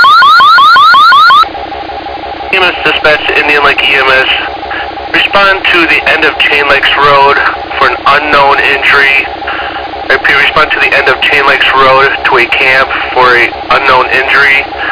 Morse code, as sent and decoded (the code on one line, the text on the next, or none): none